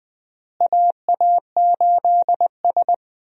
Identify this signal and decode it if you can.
Morse code, heard as a keyed tone, about 20 wpm, 700 Hz. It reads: AA8S
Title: USA Call Signs - 4 Characters - Repeated 3x - 20wpm